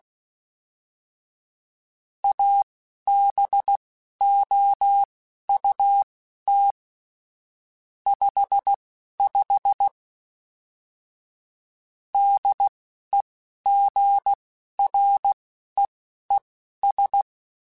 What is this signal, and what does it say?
Morse code, heard as a keyed tone, about 16 wpm, spaced as for 11 wpm, 800 Hz. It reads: ABOUT 55 DEGREES
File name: Weather Sets / Weather Sets-mp3s / Weather 4 24 @10